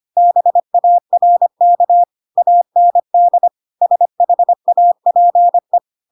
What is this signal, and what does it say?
Morse code, heard as a keyed tone, about 25 wpm, 700 Hz. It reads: BARK AND SHAPE
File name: Pinocchio / Pinocchio ch1 25 25 ik1hge